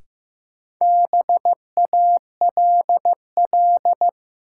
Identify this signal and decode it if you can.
Morse code, heard as a keyed tone, about 15 wpm, 700 Hz. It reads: BALL
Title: Sentences from Top 900 Words - New Word Drill - 15wpm